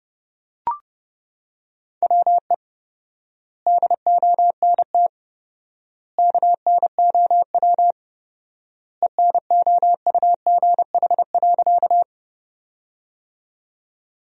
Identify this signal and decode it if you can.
Morse code, heard as a keyed tone, about 30 wpm, 700 Hz. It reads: WE DONT KNOW ENOUGH.
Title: 4x Word-Spacing - Sentences from Top 400 Words - 30wpm